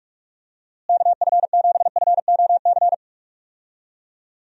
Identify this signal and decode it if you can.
Morse code, heard as a keyed tone, about 41 wpm, 700 Hz. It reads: KF7FOC
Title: USA Call Signs - 6 Characters - Repeated 2x - 45wpm